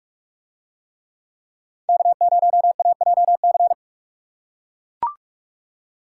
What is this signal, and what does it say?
Morse code, heard as a keyed tone, about 42 wpm, 700 Hz. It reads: K0AJC